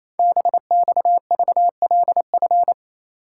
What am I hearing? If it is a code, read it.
Morse code, heard as a keyed tone, about 29 wpm, 700 Hz. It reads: BXVLF
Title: Code Groups - 5 Letters - 28wpm